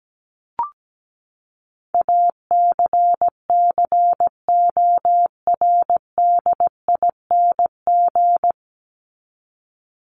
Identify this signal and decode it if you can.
Morse code, heard as a keyed tone, about 17 wpm, 700 Hz. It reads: ACCORDING
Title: LY Suffix - Top 100 Words with Base Words 17wpm